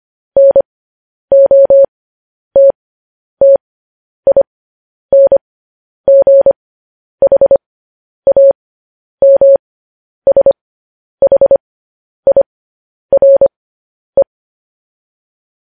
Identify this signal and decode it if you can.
Morse code, heard as a keyed tone, about 25 wpm, spaced as for 10 wpm, 550 Hz. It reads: NOTTINGHAMSHIRE
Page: news-headlines-in-morse-code-at-10-wpm